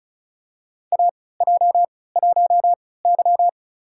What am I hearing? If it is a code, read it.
Morse code, heard as a keyed tone, about 36 wpm, spaced as for 20 wpm, 700 Hz. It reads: AJ1Y
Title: USA Call Signs - 4 Characters - 3x Character Spacing - 35wpm